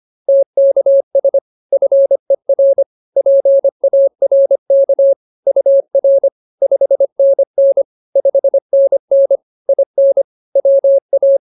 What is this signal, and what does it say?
Morse code, heard as a keyed tone, about 24 wpm, 550 Hz. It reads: TKS FER PARK UR 5NN 5NN IN WA
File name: POTA301a 25